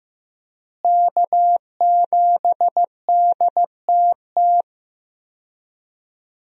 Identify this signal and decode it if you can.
Morse code, heard as a keyed tone, about 15 wpm, 700 Hz. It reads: K7DTT
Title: USA Call Signs - 5 Characters - Repeated 3x - 15wpm